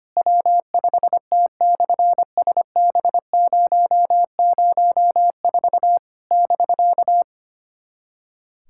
Morse code, heard as a keyed tone, about 25 wpm, 700 Hz. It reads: W5T/SB004 <BK>
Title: SOTA QSO Practice 25wpm